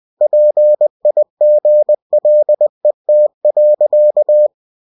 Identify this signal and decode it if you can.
Morse code, heard as a keyed tone, about 20 wpm, 600 Hz. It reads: PIGLET.
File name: cw-books / winnie-the-pooh / winnie-20wpm-page-0108